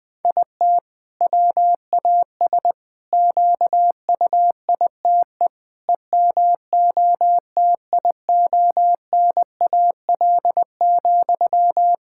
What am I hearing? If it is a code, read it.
Morse code, heard as a keyed tone, about 21 wpm, 700 Hz. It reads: IT WAS QUITE EMOTIONAL,
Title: Sentences from Top 1000 Words - Review Part 5 - 20wpm